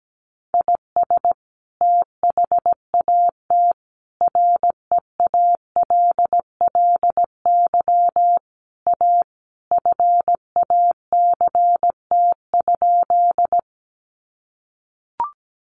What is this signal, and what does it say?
Morse code, heard as a keyed tone, about 17 wpm, 700 Hz. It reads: IS THAT REALLY A FACT?
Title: Sentences from Top 600 Words 17wpm